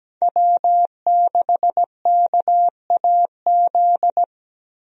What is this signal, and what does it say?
Morse code, heard as a keyed tone, about 17 wpm, 700 Hz. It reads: W6KAZ
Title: USA Call Signs - 5 Characters - Repeated 3x - 17wpm